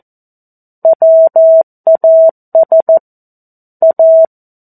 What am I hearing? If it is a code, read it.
Morse code, heard as a keyed tone, about 14 wpm, 650 Hz. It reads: WAS A
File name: 02-Chapter-02 14 WPM